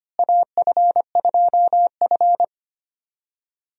Morse code, heard as a keyed tone, about 25 wpm, 700 Hz. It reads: AF2F
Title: USA Call Signs - 4 Characters - Repeated 3x - 25wpm